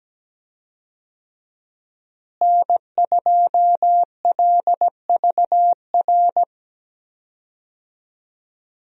Morse code, heard as a keyed tone, about 17 wpm, 700 Hz. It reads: N2LVR
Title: USA Call Signs - 5 Characters - Repeated 3x - 17wpm